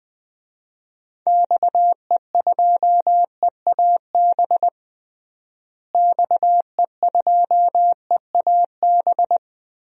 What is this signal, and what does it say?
Morse code, heard as a keyed tone, about 21 wpm, 700 Hz. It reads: XE2EAB XE2EAB
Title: DXCC-100 Call Signs - 6 Characters - Repeated 3x - 20wpm